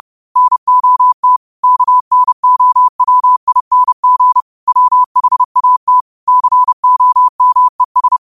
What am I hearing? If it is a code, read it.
Morse code, heard as a keyed tone, about 30 wpm, 1000 Hz. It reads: NOT KNOWING WHAT COMES